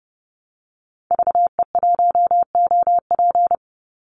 Morse code, heard as a keyed tone, about 30 wpm, 700 Hz. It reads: VE1OP
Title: Call Signs Repeated 2x - 30wpm